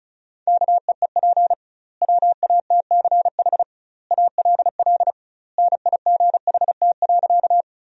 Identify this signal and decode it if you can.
Morse code, heard as a keyed tone, about 36 wpm, 700 Hz. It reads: KEEP WATCH ALL NIGHT.
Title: 2x Word-Spacing - Sentences from Top 300 Words - 35wpm